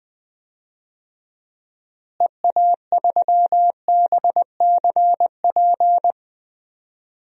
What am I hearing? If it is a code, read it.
Morse code, heard as a keyed tone, about 20 wpm, 700 Hz. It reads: EA3BCP
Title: DXCC-100 Call Signs - 6 Characters - Repeated 3x - 20wpm